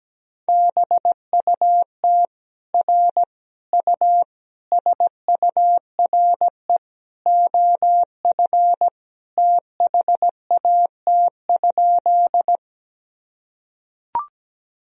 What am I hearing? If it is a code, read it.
Morse code, heard as a keyed tone, about 17 wpm, 700 Hz. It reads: BUT R U SURE OF THAT?
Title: Sentences from Top 300 Words and Abbreviations 17wpm